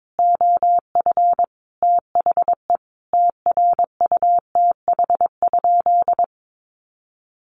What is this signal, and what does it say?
Morse code, heard as a keyed tone, about 22 wpm, 700 Hz. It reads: OF THE TRUTH?
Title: Sentences from Top 1400 Words 22wpm